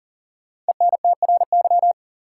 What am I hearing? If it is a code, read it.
Morse code, heard as a keyed tone, about 39 wpm, 700 Hz. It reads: ENTRY
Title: Sentences from Top 1500 Words - New Word Drill - 40wpm